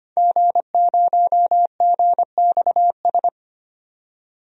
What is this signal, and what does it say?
Morse code, heard as a keyed tone, about 25 wpm, 700 Hz. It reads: G0GXS
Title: DXCC-100 Call Signs - 5 Characters - 25wpm